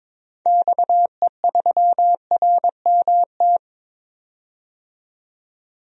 Morse code, heard as a keyed tone, about 22 wpm, 700 Hz. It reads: XE3RMT
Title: DXCC-100 Call Signs - 6 Characters - 22wpm